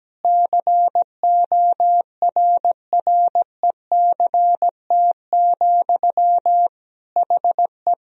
Morse code, heard as a keyed tone, about 17 wpm, 700 Hz. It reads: CORRECT, HE
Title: Sentences from Top 1000 Words - Review Part 2 - 17wpm